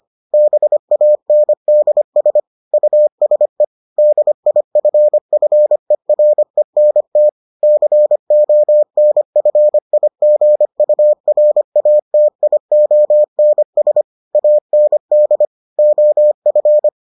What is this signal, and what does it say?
Morse code, heard as a keyed tone, about 25 wpm, 600 Hz. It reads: BANDS USE DIFFERENT CONFIGURATIONS AND OF